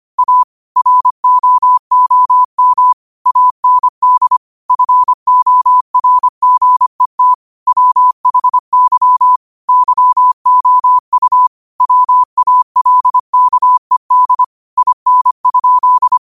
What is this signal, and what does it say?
Morse code, heard as a keyed tone, about 25 wpm, 1000 Hz. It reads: A ROOM AND FORGET WHY YOU WALKED IN?
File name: qotdfiles / 25wpm